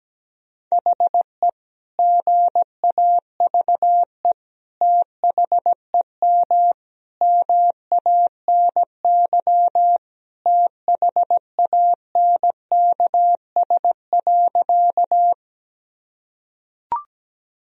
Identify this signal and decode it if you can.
Morse code, heard as a keyed tone, about 17 wpm, 700 Hz. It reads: HE GAVE THEM MANY THANKS.
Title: Sentences from Top 1000 Words - Review Part 3 - 17wpm